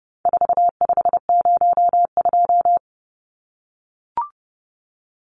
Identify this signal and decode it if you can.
Morse code, heard as a keyed tone, about 30 wpm, 700 Hz. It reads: 4502
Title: Numbers - 4-Digit 30wpm